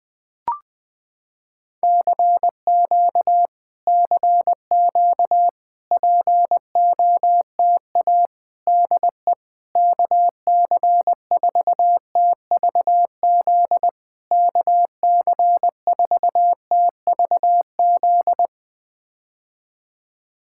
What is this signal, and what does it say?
Morse code, heard as a keyed tone, about 20 wpm, 700 Hz. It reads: CQ CQ POTA DE KC4TVZ KC4TVZ
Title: POTA QSO Practice - 20wpm